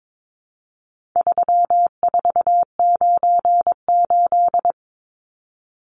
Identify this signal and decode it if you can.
Morse code, heard as a keyed tone, about 22 wpm, 700 Hz. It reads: 3498
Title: Numbers - 4-Digit 22wpm